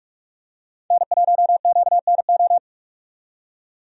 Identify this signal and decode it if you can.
Morse code, heard as a keyed tone, about 45 wpm, 700 Hz. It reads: N1QNO
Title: USA Call Signs - 5 Characters - 45wpm